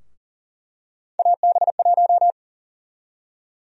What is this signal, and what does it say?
Morse code, heard as a keyed tone, about 40 wpm, 700 Hz. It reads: AB1
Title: USA Call Sign Prefixes - Repeated 2x - 40wpm